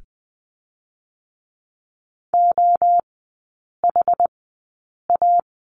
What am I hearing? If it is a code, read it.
Morse code, heard as a keyed tone, about 20 wpm, spaced as for 8 wpm, 700 Hz. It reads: OHA
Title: 107 Lesson 18e - Letter m - Three Characters 20wpm